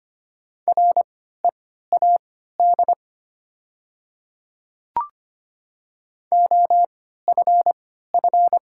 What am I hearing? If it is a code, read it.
Morse code, heard as a keyed tone, about 25 wpm, spaced as for 14 wpm, 690 Hz. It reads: READ OFF
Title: 3x Character Spacing - Top 300 Words - 25wpm